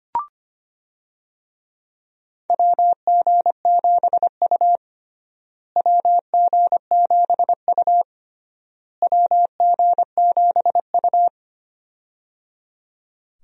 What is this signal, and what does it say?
Morse code, heard as a keyed tone, about 25 wpm, 700 Hz. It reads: WG7U WG7U WG7U
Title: USA Call Signs - 4 Characters - Repeated 3x - 25wpm